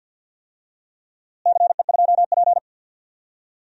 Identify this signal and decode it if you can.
Morse code, heard as a keyed tone, about 52 wpm, 700 Hz. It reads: CE2P
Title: DXCC-100 Call Signs - 4 Characters - Repeated 3x - 50wpm